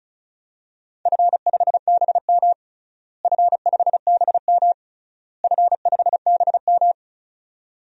Morse code, heard as a keyed tone, about 36 wpm, 700 Hz. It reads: F5BM F5BM F5BM
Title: DXCC-100 Call Signs - 4 Characters - Repeated 3x - 35wpm